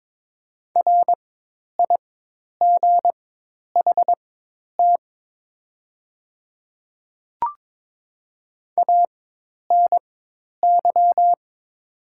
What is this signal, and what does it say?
Morse code, heard as a keyed tone, about 22 wpm, spaced as for 10 wpm, 700 Hz. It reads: RIGHT ANY
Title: 4x Character Spacing - Top 300 Words - 22wpm